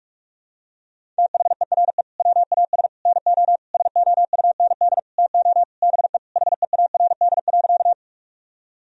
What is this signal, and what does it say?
Morse code, heard as a keyed tone, about 42 wpm, 700 Hz. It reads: THERE WAS NO SOUND TO BE HEARD.
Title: Sentences from Top 1000 Words - Review Part 1 - 45wpm